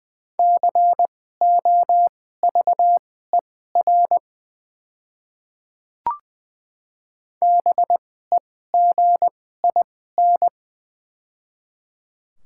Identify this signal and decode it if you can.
Morse code, heard as a keyed tone, about 20 wpm, spaced as for 14 wpm, 690 Hz. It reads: COVER BEGIN
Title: 2x Character Spacing - Top 300 Words - 20wpm